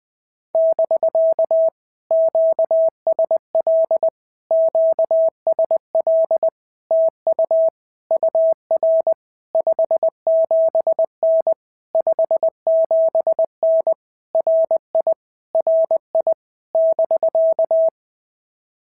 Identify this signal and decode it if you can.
Morse code, heard as a keyed tone, about 20 wpm, 650 Hz. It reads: <BK> QSL QSL TU UR 57N 57N RI RI <BK>
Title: POTA QSO Practice - 20wpm